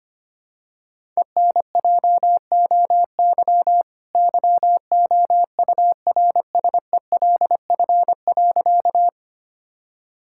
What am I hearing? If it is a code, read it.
Morse code, heard as a keyed tone, about 25 wpm, 700 Hz. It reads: ENJOY YOURSELF.
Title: Sentences from Top 1000 Words - Review Part 5 - 25wpm